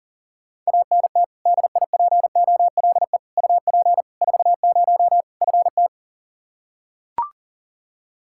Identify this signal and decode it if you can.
Morse code, heard as a keyed tone, about 40 wpm, 700 Hz. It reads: ANT DIPOLE UP 40 FT
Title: Practice QSO - Exchange 2 - Info - 40wpm